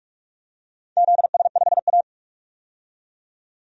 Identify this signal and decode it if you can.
Morse code, heard as a keyed tone, about 44 wpm, 700 Hz. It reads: ZS5A
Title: DXCC-100 Call Signs - 4 Characters - 45wpm